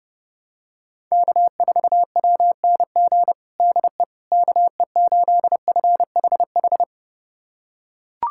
Morse code, heard as a keyed tone, about 30 wpm, 700 Hz. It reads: K4WNG DE KE8FHH
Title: USA Callsign DE Callsign 30wpm